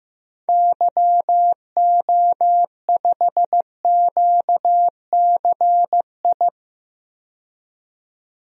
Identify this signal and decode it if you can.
Morse code, heard as a keyed tone, about 15 wpm, 700 Hz. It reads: YO5QCI
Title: DXCC-100 Call Signs - 6 Characters - 15wpm